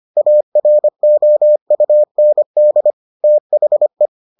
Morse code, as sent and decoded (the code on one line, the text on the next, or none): .- .-. --- ..- -. -.. / - .... .
AROUND THE